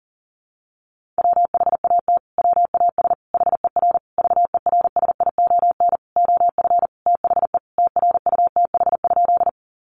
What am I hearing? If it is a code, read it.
Morse code, heard as a keyed tone, about 40 wpm, 700 Hz. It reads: WHAT WAS HER VERSION OF THE TRUTH?
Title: Sentences from Top 1400 Words 40wpm